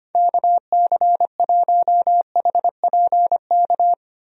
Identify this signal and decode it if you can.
Morse code, heard as a keyed tone, about 25 wpm, 700 Hz. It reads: KC1HPK
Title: USA Call Signs - 6 Characters - 25wpm